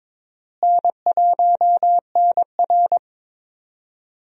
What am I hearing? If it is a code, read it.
Morse code, heard as a keyed tone, about 21 wpm, 700 Hz. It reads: N1NR
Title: USA Call Signs - 4 Characters - Repeated 2x - 22wpm